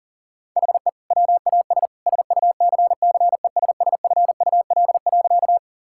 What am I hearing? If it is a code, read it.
Morse code, heard as a keyed tone, about 40 wpm, 700 Hz. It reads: HE WAS SUCCESSFUL.
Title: Sentences from Top 1000 Words - Review Part 6 - 40wpm